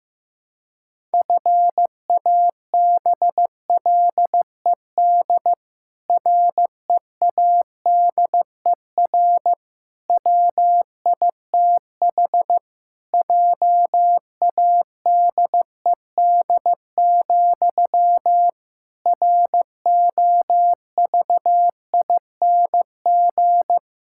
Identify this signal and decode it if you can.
Morse code, heard as a keyed tone, about 15 wpm, 700 Hz. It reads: FABLED READER WITH JADED, ROVING
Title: Pangram Challenge v2 - 15wpm